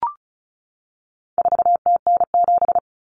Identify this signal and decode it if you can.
Morse code, heard as a keyed tone, about 34 wpm, 700 Hz. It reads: 4TN7
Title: Common Cut Numbers - 4-Digit 35wpm